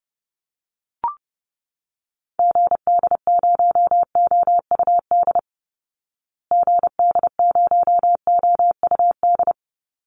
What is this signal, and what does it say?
Morse code, heard as a keyed tone, about 30 wpm, 700 Hz. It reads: GD0OUD GD0OUD
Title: Call Signs Repeated 2x - 30wpm